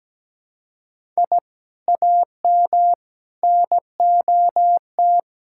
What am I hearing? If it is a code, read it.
Morse code, heard as a keyed tone, about 17 wpm, 700 Hz. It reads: I AM NOT
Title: Sentences from Top 1000 Words - Review Part 5 - 17wpm